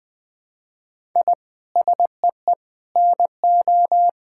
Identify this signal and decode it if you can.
Morse code, heard as a keyed tone, about 20 wpm, 700 Hz. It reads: I SEE NO
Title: Sentences from Top 1000 Words - Review Part 5 - 20wpm